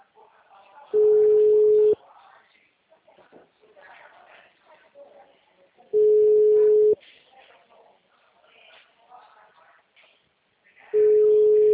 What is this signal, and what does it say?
Ringback tone (Europe, 425 Hz), with speech.
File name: out- -2049- - - .4025